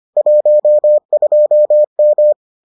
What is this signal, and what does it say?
Morse code, heard as a keyed tone, about 25 wpm, 600 Hz. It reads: 12M